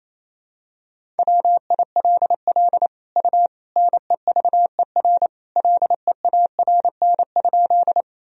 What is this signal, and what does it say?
Morse code, heard as a keyed tone, about 28 wpm, 700 Hz. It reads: WILL U NEVER LEARN?
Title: Sentences from Top 300 Words and Abbreviations 28wpm